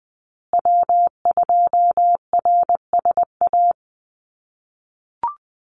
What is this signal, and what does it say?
Morse code, heard as a keyed tone, about 20 wpm, 700 Hz. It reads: W2RSA
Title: USA Call Signs Repeated 3x 20wpm